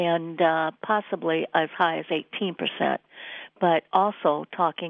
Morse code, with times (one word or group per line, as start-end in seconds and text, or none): none